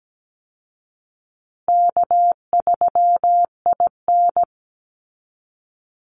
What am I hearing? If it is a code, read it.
Morse code, heard as a keyed tone, about 17 wpm, 700 Hz. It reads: K3IN
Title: Easy Call Signs Repeated 3x - 17wpm